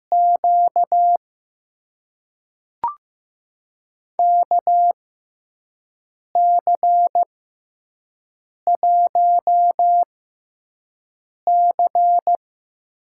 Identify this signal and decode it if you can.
Morse code, heard as a keyed tone, about 15 wpm, spaced as for 5 wpm, 700 Hz. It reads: Q KC1C
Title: USA Call Signs - 6 Characters - 6x Character Spacing - 15wpm